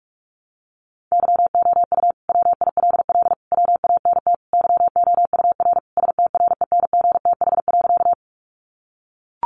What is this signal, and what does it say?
Morse code, heard as a keyed tone, about 44 wpm, 700 Hz. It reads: YOU WILL WANT YOUR STRENGTH.
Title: Sentences from Top 1300 Words 45wpm